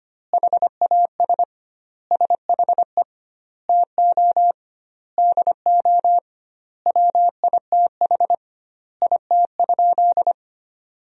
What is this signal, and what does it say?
Morse code, heard as a keyed tone, about 25 wpm, 700 Hz. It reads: HAS SHE TO DO WITH IT?
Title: 2x Word-Spacing - Sentences from Top 100 Words - 25wpm